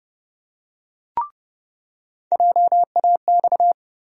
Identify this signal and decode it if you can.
Morse code, heard as a keyed tone, about 30 wpm, 700 Hz. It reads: JAX
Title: USA Call Sign Postfixes - 3 Letters 30wpm